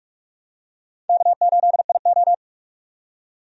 Morse code, heard as a keyed tone, about 46 wpm, 700 Hz. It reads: K8IO
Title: USA Call Signs - 4 Characters - 45wpm